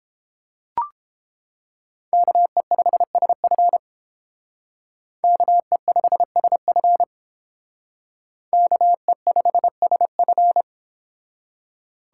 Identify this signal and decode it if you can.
Morse code, heard as a keyed tone, about 30 wpm, 700 Hz. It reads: KE5SF KE5SF KE5SF